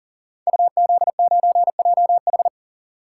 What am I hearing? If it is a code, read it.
Morse code, heard as a keyed tone, about 39 wpm, 700 Hz. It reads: UZ9JH
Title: Code Groups - 5 Letter:Numbers - 40wpm